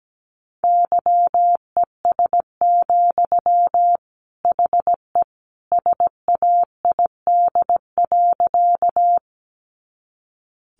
Morse code, heard as a keyed tone, about 17 wpm, 700 Hz. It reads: YES, HE SAID.
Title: Sentences from Top 500 Words 17wpm